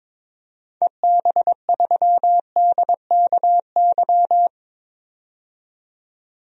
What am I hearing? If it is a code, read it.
Morse code, heard as a keyed tone, about 22 wpm, 700 Hz. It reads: EB3DKY
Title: DXCC-100 Call Signs - 6 Characters - 22wpm